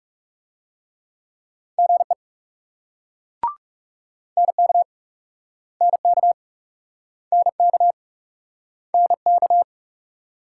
Morse code, heard as a keyed tone, about 43 wpm, 700 Hz. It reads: GE NK NK NK NK